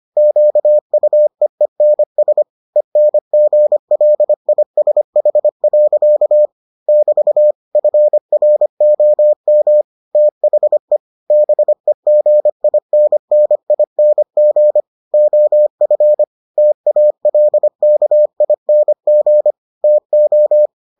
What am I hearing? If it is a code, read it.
Morse code, heard as a keyed tone, about 25 wpm, 600 Hz. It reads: QUEENS ENGLISH. <BT> FROM THE BEGINNING OF TALKING TO